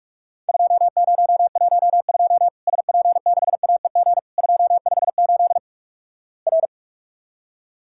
Morse code, heard as a keyed tone, about 44 wpm, 700 Hz. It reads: 2012 SP6AEG 258 R
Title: Worked All Europe DX Contest - QTC Exchange - 45wpm